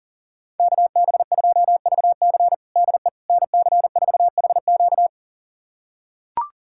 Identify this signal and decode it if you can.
Morse code, heard as a keyed tone, about 40 wpm, 700 Hz. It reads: KB2VC DE NC4HQ
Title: USA Callsign DE Callsign - 5 Characters - 40wpm